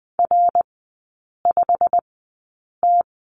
Morse code, heard as a keyed tone, about 19 wpm, spaced as for 8 wpm, 690 Hz. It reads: R5T
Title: 212 Lesson 34e - Number 8 - Three Characters 20wpm